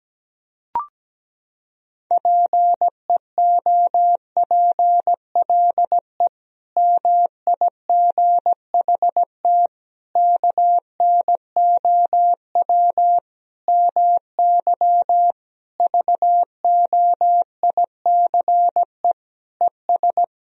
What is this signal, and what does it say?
Morse code, heard as a keyed tone, about 17 wpm, 700 Hz. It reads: PEOPLE MIGHT KNOW MY VOICE ES
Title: Sentences from Top 500 Words and Abbreviations 17wpm